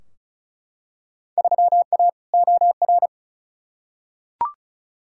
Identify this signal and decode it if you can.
Morse code, heard as a keyed tone, about 36 wpm, 700 Hz. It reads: 3A OR
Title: Field Day Exchanges - 35wpm